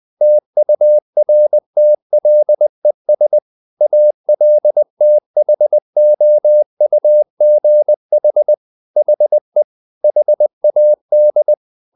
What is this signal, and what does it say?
Morse code, heard as a keyed tone, about 20 wpm, 600 Hz. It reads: TURTLES ALTHOUGH HE HAD
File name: the old man and the sea / the old man and the sea part 1 20wpm